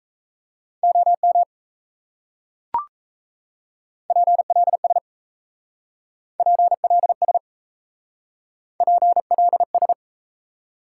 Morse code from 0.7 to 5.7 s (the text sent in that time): OM PLS